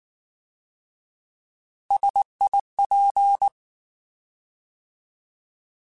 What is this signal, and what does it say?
Morse code, heard as a keyed tone, about 19 wpm, 800 Hz. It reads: SIP
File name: Alphabetical / Alphabetical mp3s / Ss @18